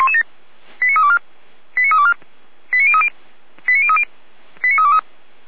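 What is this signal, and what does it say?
Five-tone selective call (ZVEI), heard as quick runs of five tones, tones of about 70 ms. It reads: 89324 89324 89030 89030 89323
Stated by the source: Five-tone ZVEI selective call received by SDR.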